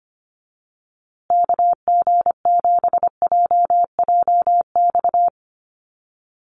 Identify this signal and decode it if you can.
Morse code, heard as a keyed tone, about 25 wpm, 700 Hz. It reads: KG7JJX
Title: USA Call Signs Repeated 2x 25wpm